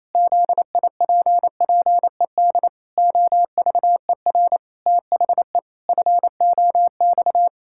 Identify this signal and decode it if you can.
Morse code, heard as a keyed tone, about 28 wpm, 700 Hz. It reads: ZIPPED OVER THE FOX
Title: Pangram Challenge v2 - 28wpm